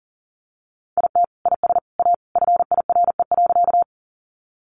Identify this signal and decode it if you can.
Morse code, heard as a keyed tone, about 40 wpm, 700 Hz. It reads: IT IS A FIRE.
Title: Sentences from Top 400 Words 40wpm